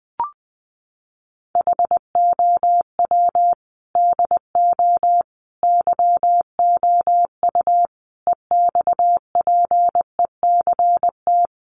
Morse code, heard as a keyed tone, about 20 wpm, 700 Hz. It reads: HOW DO YOU EXPECT
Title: Sentences from Top 1100 Words 20wpm